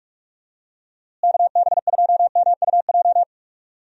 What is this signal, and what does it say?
Morse code, heard as a keyed tone, about 46 wpm, 700 Hz. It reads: KB2MUJ
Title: USA Call Signs - 6 Characters - Repeated 3x - 45wpm